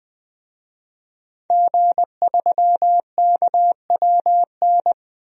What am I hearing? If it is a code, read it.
Morse code, heard as a keyed tone, about 21 wpm, 700 Hz. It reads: G3KWN
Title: DXCC-100 Call Signs - 5 Characters - 20wpm